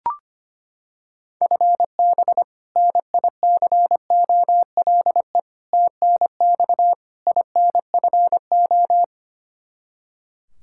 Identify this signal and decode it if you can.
Morse code, heard as a keyed tone, about 25 wpm, 700 Hz. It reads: FB NICOLE TNX INFO
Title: Practice QSO - Exchange 3 - More Info - 25wpm